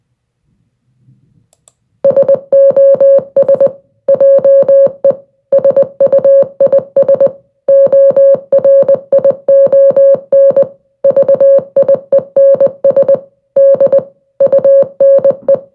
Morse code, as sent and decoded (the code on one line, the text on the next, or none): ... --- ... / .--- . / ... ..- .. ... / --- .-. .. --- -. / ...- .. -. ... / -.. / ..- -. .
SOS JE SUIS ORION VIENS D UNE